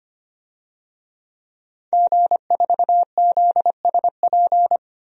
Morse code, heard as a keyed tone, about 25 wpm, 700 Hz. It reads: G4ZSP
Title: DXCC-100 Call Signs - 5 Characters - 25wpm